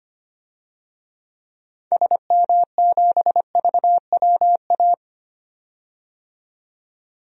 Morse code, heard as a keyed tone, about 25 wpm, 690 Hz. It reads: SM7VWA
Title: DXCC-100 Call Signs - 6 Characters - 25wpm